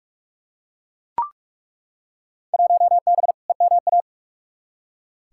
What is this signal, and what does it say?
Morse code, heard as a keyed tone, about 43 wpm, 700 Hz. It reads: 1B EMA
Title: Field Day Exchanges - 45wpm